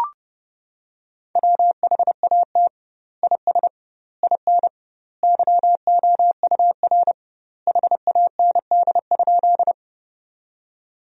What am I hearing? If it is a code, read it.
Morse code, heard as a keyed tone, about 32 wpm, 700 Hz. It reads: WHAT IS IN YOUR HAND?
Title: 2x Word-Spacing - Sentences from Top 200 Words - 30wpm